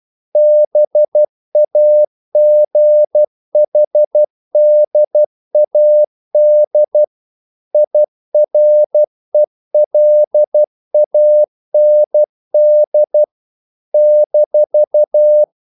Morse code, heard as a keyed tone, about 12 wpm, 600 Hz. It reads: BAGHDAD IRELAND -
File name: CW / Practice / NationalCapitals12